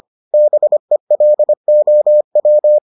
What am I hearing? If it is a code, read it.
Morse code, heard as a keyed tone, about 25 wpm, 600 Hz. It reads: BELOW